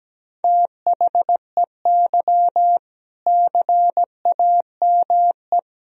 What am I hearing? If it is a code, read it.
Morse code, heard as a keyed tone, about 17 wpm, 700 Hz. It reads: THEY CAME
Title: Sentences from Top 1000 Words - Review Part 3 - 17wpm